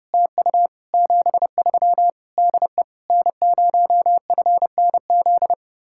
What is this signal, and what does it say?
Morse code, heard as a keyed tone, about 39 wpm, 700 Hz. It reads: TU 73 DE N0FNZ